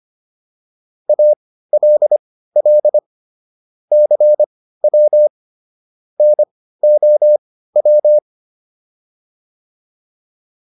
Morse code, heard as a keyed tone, about 25 wpm, spaced as for 15 wpm, 600 Hz. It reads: ALL CW NOW